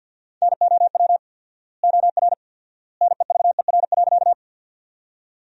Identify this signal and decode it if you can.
Morse code, heard as a keyed tone, about 50 wpm, 700 Hz. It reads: NOW OR NEVER.
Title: 4x Word-Spacing - Sentences from Top 300 Words - 50wpm